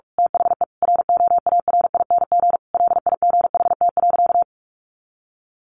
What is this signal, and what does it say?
Morse code, heard as a keyed tone, about 45 wpm, 700 Hz. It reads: THE ROARING LIGHT.
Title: A Sound Of Thunder 45wpm